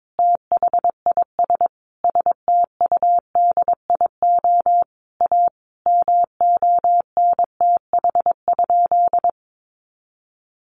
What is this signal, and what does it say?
Morse code, heard as a keyed tone, about 22 wpm, 700 Hz. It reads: THIS STUDIO A MONTH?
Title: Sentences from Top 1200 Words 22wpm